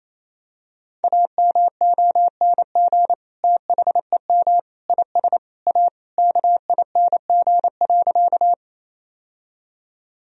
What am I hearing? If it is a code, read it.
Morse code, heard as a keyed tone, about 28 wpm, 700 Hz. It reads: AMONG THEM IS A KING.
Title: Sentences from Top 1000 Words - Review Part 3 - 28wpm